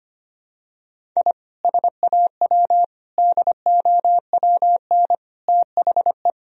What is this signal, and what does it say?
Morse code, heard as a keyed tone, about 25 wpm, 700 Hz. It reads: I SAW DOWN THE
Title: Sentences from Top 1000 Words - Review Part 1 - 25wpm